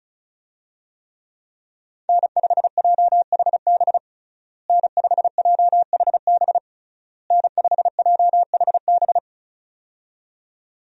Morse code, heard as a keyed tone, about 36 wpm, 700 Hz. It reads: N5JHB N5JHB N5JHB
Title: USA Call Signs - 5 Characters - Repeated 3x - 35wpm